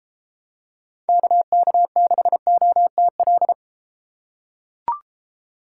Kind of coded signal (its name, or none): Morse code